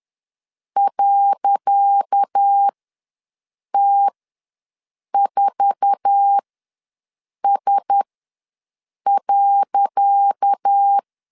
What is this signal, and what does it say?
Morse code, heard as a keyed tone, about 10 wpm, spaced as for 6 wpm, 800 Hz. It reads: .T4S.
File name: LEZIONI / 10 SIGNS / 10C DOT ws 10wpm 4213